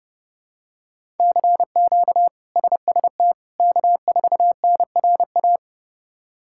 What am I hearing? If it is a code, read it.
Morse code, heard as a keyed tone, about 31 wpm, 700 Hz. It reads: CQ SST K4NRA
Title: K1USN SST QSO Practice - 30wpm